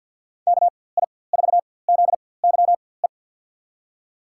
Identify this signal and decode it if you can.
Morse code, heard as a keyed tone, about 46 wpm, spaced as for 23 wpm, 700 Hz. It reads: KI4ZYE
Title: USA Call Signs - 6 Characters - 4x Character Spacing - 50wpm